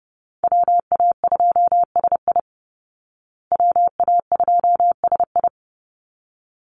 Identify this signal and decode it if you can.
Morse code, heard as a keyed tone, about 30 wpm, 700 Hz. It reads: WA2SI WA2SI
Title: Call Signs Repeated 3x - 30wpm